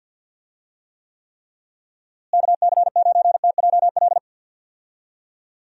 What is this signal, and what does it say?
Morse code, heard as a keyed tone, about 47 wpm, 700 Hz. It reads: KC9TJL
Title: USA Call Signs - 6 Characters - Repeated 3x - 50wpm